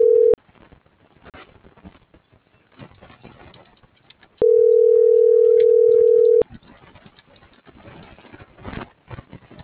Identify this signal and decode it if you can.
Ringback tone (North America), 440 + 480 Hz.
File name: CENTRAL SAGOT / out danae.bsagot